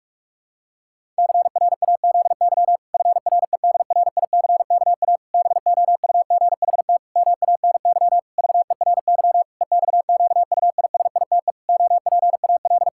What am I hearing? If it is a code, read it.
Morse code, heard as a keyed tone, about 41 wpm, 700 Hz. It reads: CRAZY FREDRICKA BOUGHT MANY VERY EXQUISITE OPAL
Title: Pangram Challenge v2 - 45wpm